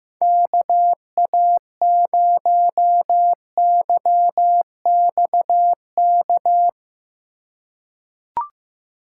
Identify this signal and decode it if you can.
Morse code, heard as a keyed tone, about 15 wpm, 700 Hz. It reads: KA0YXK
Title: USA Call Signs - 3x Character Spacing - 15wpm